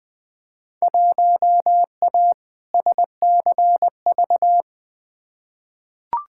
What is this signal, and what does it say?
Morse code, heard as a keyed tone, about 21 wpm, 700 Hz. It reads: 1A SCV
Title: Field Day Exchanges - 20wpm